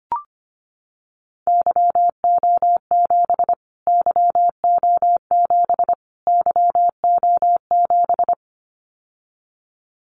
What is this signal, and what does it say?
Morse code, heard as a keyed tone, about 25 wpm, 700 Hz. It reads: YO7 YO7 YO7
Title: Callsign Prefixes for Top 25 DXCC - Repeated 3x - 25wpm